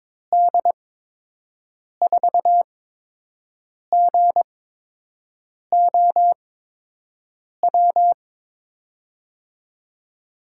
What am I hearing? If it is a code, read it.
Morse code, heard as a keyed tone, about 22 wpm, spaced as for 6 wpm, 700 Hz. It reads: D4GOW
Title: USA Call Signs - 6 Characters - 8x Character Spacing - 22wpm